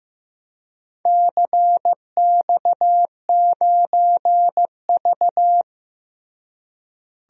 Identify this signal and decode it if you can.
Morse code, heard as a keyed tone, about 15 wpm, 700 Hz. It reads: CX9V
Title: DXCC-100 Call Signs - 4 Characters - Repeated 2x - 15wpm